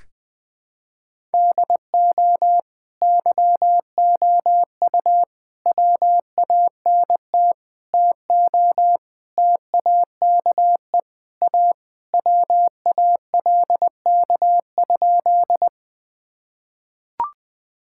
Morse code, heard as a keyed tone, about 20 wpm, 700 Hz. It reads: DO YOU WANT TO TAKE A WALK?
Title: Sentences with 3-Word Phrases - 20wpm